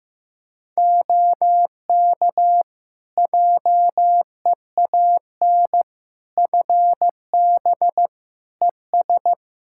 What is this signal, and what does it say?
Morse code, heard as a keyed tone, about 15 wpm, 700 Hz. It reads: OK JEAN FB ES
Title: Practice QSO - Exchange 2 - Info - 15wpm